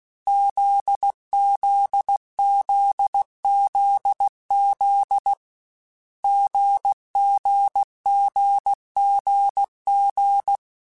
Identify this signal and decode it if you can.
Morse code, heard as a keyed tone, about 16 wpm, 800 Hz. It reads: ZZZZZ GGGGG
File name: Difficult Combos / Difficult Combos-mp3s / GZ GZ7 @15